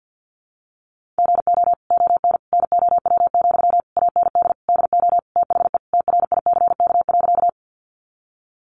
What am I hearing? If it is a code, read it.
Morse code, heard as a keyed tone, about 50 wpm, 700 Hz. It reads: GO ON NOW, AND DO THE TRICK.
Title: Sentences from Top 1500 Words 50wpm